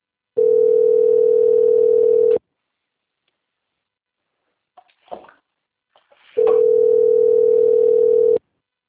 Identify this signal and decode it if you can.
Ringback tone (North America), 440 + 480 Hz.
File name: out- - - - - .27210